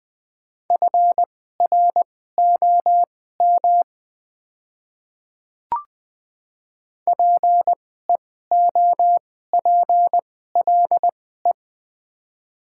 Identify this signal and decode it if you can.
Morse code, heard as a keyed tone, about 20 wpm, spaced as for 14 wpm, 700 Hz. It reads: FROM PEOPLE
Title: 2x Character Spacing - Top 100 Words - 20wpm